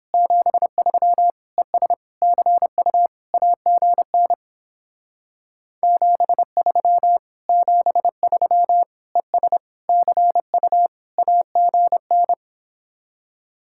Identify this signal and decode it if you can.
Morse code, heard as a keyed tone, about 29 wpm, 700 Hz. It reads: 73 ES CU AGN 73 73 ES CU AGN